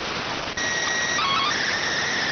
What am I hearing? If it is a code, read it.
SSTV (slow-scan television), PD 290.